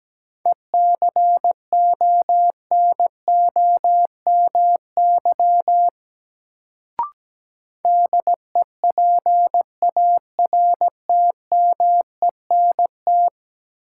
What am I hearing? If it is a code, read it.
Morse code, heard as a keyed tone, about 17 wpm, 700 Hz. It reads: ECONOMY DEPARTMENT
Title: Sentences from Top 1200 Words - New Word Drill Alt - 17wpm